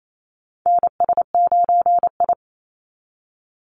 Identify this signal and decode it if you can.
Morse code, heard as a keyed tone, about 28 wpm, 700 Hz. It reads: NS9I
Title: Call Signs - 28wpm